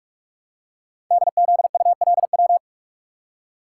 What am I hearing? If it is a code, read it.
Morse code, heard as a keyed tone, about 44 wpm, 700 Hz. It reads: DZULW
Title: Code Groups - 5 Letters - 45wpm